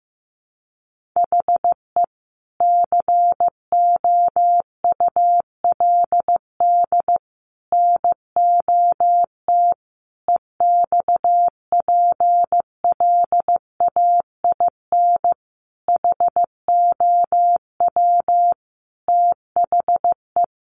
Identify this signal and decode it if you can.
Morse code, heard as a keyed tone, about 15 wpm, 700 Hz. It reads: HE COULD NOT EXPLAIN HOW THE
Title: Sentences from Top 1500 Words 15wpm